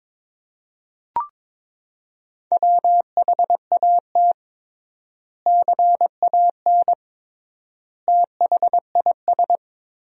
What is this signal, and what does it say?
Morse code, heard as a keyed tone, about 22 wpm, 700 Hz. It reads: WHAT CAN THIS
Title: 3x Word-Spacing - Sentences from Top 100 Words - 22wpm